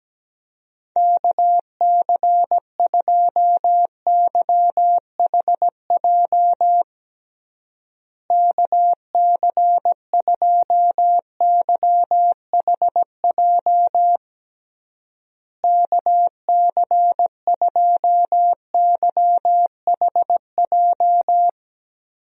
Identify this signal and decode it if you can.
Morse code, heard as a keyed tone, about 17 wpm, 700 Hz. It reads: KC2YHJ KC2YHJ KC2YHJ
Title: USA Call Signs - 6 Characters - Repeated 3x - 17wpm